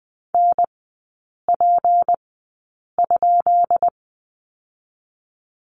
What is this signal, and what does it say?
Morse code, heard as a keyed tone, about 20 wpm, spaced as for 8 wpm, 700 Hz. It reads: NP?
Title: 198 Lesson 32e - Letter k - Three Characters 20wpm